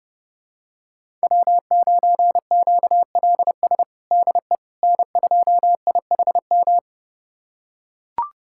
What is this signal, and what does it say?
Morse code, heard as a keyed tone, about 32 wpm, 700 Hz. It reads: W9QLS DE N2IHM
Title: USA Callsign DE Callsign - 5 Characters - 30wpm